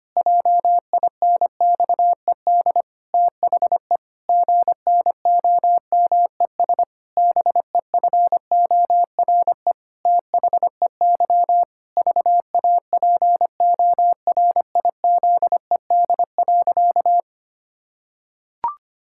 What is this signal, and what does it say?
Morse code, heard as a keyed tone, about 25 wpm, 700 Hz. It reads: JINXED THE GNOMES BEFORE THEY VAPORIZED.
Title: Pangram Challenge v2 - 25wpm